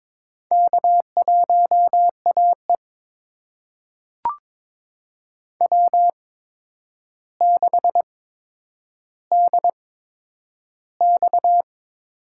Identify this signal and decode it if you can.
Morse code, heard as a keyed tone, about 22 wpm, spaced as for 9 wpm, 700 Hz. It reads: K1AE W6DX
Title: USA Call Signs - 4 Characters - 8x Character Spacing - 22wpm